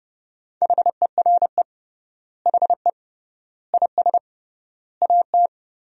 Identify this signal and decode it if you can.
Morse code, heard as a keyed tone, about 30 wpm, 700 Hz. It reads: HERE HE IS AT
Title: 3x Word-Spacing - Sentences from Top 300 Words - 30wpm